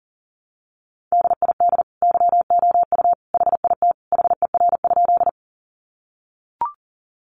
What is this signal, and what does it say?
Morse code, heard as a keyed tone, about 40 wpm, 700 Hz. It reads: DID YOU HIT HER?
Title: Sentences from Top 1200 Words 40wpm